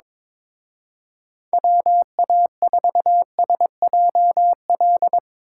Morse code, heard as a keyed tone, about 22 wpm, 700 Hz. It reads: WA4SJL
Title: Speed Racing - USA 2x3 Call Signs 22wpm